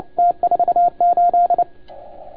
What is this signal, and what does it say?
Morse code, heard as a keyed tone, about 29 wpm, 690 Hz, with radio noise. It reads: T48
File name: Archiv / CUC287 / OK1FGD CUC 287